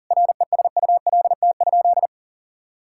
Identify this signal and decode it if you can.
Morse code, heard as a keyed tone, about 40 wpm, 700 Hz. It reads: RESULT?
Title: Sentences from Top 1000 Words - Review Part 5 - 40wpm